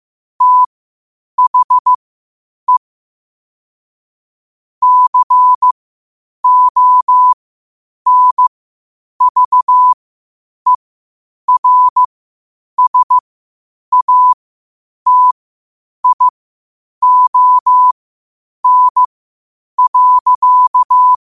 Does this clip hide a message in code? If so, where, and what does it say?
Yes: Morse code from 0.4 to 21.1 s, heard as a keyed tone, about 15 wpm, spaced as for 8 wpm, 1000 Hz. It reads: THE CONVERSATION.